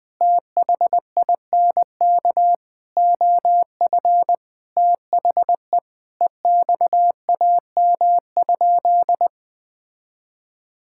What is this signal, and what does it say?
Morse code, heard as a keyed tone, about 20 wpm, 700 Hz. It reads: THINK OF THE EXAM?
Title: Sentences from Top 1000 Words - Review Part 5 - 20wpm